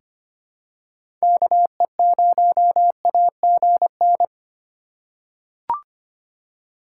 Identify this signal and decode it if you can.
Morse code, heard as a keyed tone, about 25 wpm, 700 Hz. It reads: KE0AGN
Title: USA Call Signs - 6 Characters - 4x Character Spacing - 25wpm